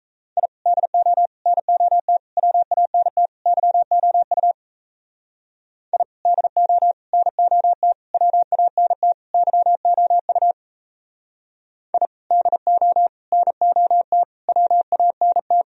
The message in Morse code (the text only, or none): I DO NOT WANT YOU I DO NOT WANT YOU I DO NOT WANT